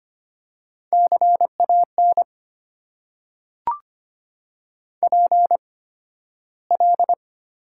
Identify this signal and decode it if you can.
Morse code, heard as a keyed tone, about 25 wpm, spaced as for 13 wpm, 700 Hz. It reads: CAN PL
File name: Top 90 QSO Elements - 8x Character Spacing - 25wpm